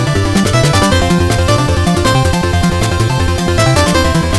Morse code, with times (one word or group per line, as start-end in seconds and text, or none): none